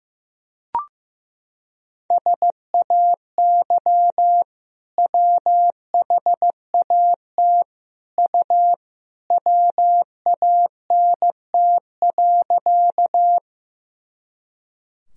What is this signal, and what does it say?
Morse code, heard as a keyed tone, about 15 wpm, 700 Hz. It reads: SAY WHAT U WANT.
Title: Sentences from Top 200 Words and Abbreviations 15wpm